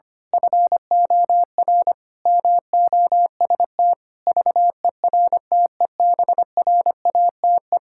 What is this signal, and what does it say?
Morse code, heard as a keyed tone, about 25 wpm, 700 Hz. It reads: FOR MOST VERTEBRATE
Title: Pangram Challenge v2 - 25wpm